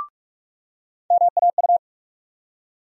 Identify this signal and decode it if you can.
Morse code, heard as a keyed tone, about 48 wpm, 700 Hz. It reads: MAU
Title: USA Call Sign Postfixes - 3 Letters 45wpm